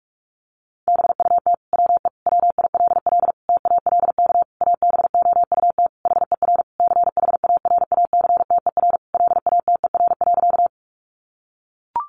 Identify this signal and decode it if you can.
Morse code, heard as a keyed tone, about 44 wpm, 700 Hz. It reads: BUT WE WILL TALK ABOUT HER CHARACTER LATER.
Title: Sentences from Top 1200 Words 45wpm